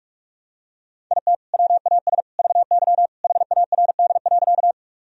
Morse code, heard as a keyed tone, about 46 wpm, 700 Hz. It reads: IT WAS VY HARD.
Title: Sentences from Top 300 Words and Abbreviations 45wpm